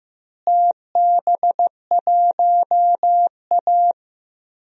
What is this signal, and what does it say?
Morse code, heard as a keyed tone, about 16 wpm, 700 Hz. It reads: TB1A
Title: DXCC-100 Call Signs - 4 Characters - Repeated 3x - 15wpm